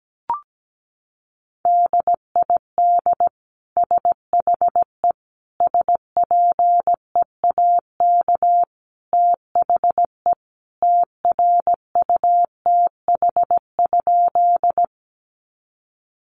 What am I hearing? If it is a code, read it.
Morse code, heard as a keyed tone, about 17 wpm, 700 Hz. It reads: DID SHE SPEAK THE TRUTH?
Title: Sentences from Top 1100 Words 17wpm